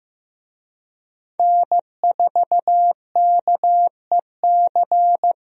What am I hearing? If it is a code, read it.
Morse code, heard as a keyed tone, about 16 wpm, 700 Hz. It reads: N4KEC
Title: USA Call Signs - 5 Characters - Repeated 3x - 15wpm